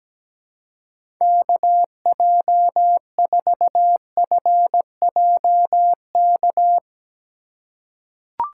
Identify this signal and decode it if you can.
Morse code, heard as a keyed tone, about 17 wpm, 700 Hz. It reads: KJ4FJK
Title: USA Call Signs - 6 Characters - 4x Character Spacing - 17wpm